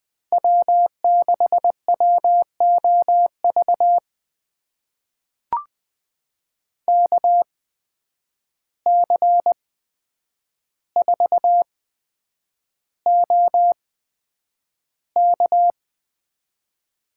Morse code, heard as a keyed tone, about 20 wpm, spaced as for 9 wpm, 700 Hz. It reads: W6WOV KC4OK
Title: USA Call Signs - 5 Characters - 8x Character Spacing - 20wpm